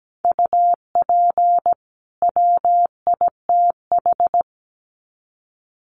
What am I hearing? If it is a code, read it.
Morse code, heard as a keyed tone, about 17 wpm, 700 Hz. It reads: UP WITH
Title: Sets of 2 Words 17wpm